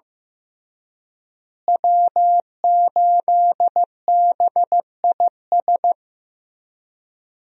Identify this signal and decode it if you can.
Morse code, heard as a keyed tone, about 15 wpm, 700 Hz. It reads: W8BIS
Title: USA Call Signs - 5 Characters - Repeated 3x - 15wpm